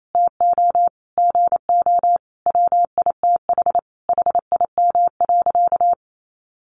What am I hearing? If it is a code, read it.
Morse code, heard as a keyed tone, about 28 wpm, 700 Hz. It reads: TO GO WITH HIM.
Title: Sentences from Top 1300 Words 28wpm